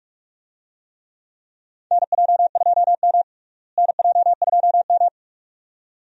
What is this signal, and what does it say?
Morse code, heard as a keyed tone, about 46 wpm, 700 Hz. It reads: NJ2M NJ2M
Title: USA Call Signs - 4 Characters - Repeated 2x - 45wpm